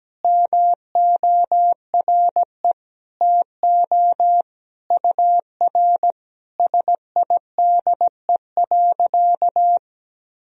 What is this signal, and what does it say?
Morse code, heard as a keyed tone, about 17 wpm, 700 Hz. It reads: MORE TO UR SIDE.
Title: Sentences from Top 400 Words and Abbreviations 17wpm